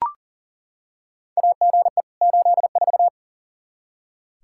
Morse code, heard as a keyed tone, about 40 wpm, 700 Hz. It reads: AGE 84
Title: Practice QSO - Exchange 3 - More Info - 40wpm